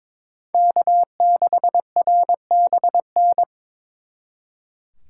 Morse code, heard as a keyed tone, about 26 wpm, 700 Hz. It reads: K6RBN
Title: Speed Racing - USA 1x3 Call Signs 22wpm